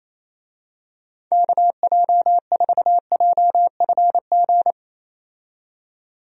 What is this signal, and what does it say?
Morse code, heard as a keyed tone, about 28 wpm, 700 Hz. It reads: KJ4JFG
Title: USA Call Signs - 6 Characters - 28wpm